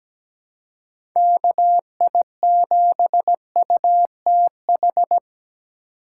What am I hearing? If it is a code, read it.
Morse code, heard as a keyed tone, about 17 wpm, 700 Hz. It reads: KI7UTH
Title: USA Call Signs - 6 Characters - 3x Character Spacing - 17wpm